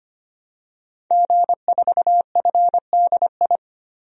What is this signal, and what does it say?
Morse code, heard as a keyed tone, about 25 wpm, 700 Hz. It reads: G4FDI
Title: DXCC-100 Call Signs - 5 Characters - 25wpm